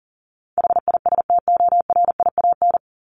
Morse code, heard as a keyed tone, about 40 wpm, 700 Hz. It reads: HISTORIAN
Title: Top 1751-2000 Words 40wpm